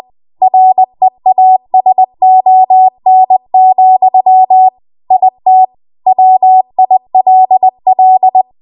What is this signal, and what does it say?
Morse code, heard as a keyed tone, about 20 wpm, 750 Hz. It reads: REASON, IT WILL
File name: Morse / 20WPM